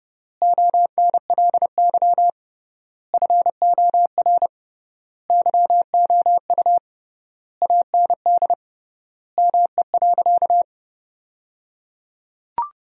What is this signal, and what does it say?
Morse code, heard as a keyed tone, about 30 wpm, 700 Hz. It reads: ONLY FOR YOU AND ME.
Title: 3x Word-Spacing - Sentences from Top 500 Words - 30wpm